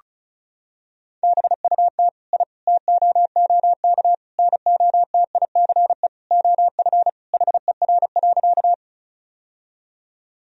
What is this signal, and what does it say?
Morse code, heard as a keyed tone, about 34 wpm, 700 Hz. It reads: BUT I TOOK NOTICE OF HER.
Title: Sentences from Top 1000 Words - Review Part 2 - 35wpm